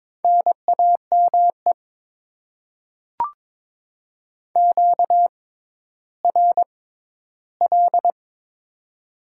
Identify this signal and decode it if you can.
Morse code, heard as a keyed tone, about 22 wpm, spaced as for 12 wpm, 700 Hz. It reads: NAME QRL
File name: Top 20 QSO Elements - 6x Character Spacing - 22wpm